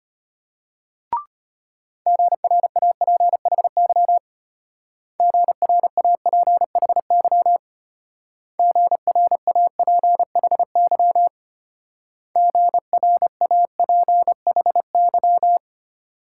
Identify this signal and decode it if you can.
Morse code, heard as a keyed tone, about 32 wpm, 700 Hz. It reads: GRAPHY GRAPHY GRAPHY GRAPHY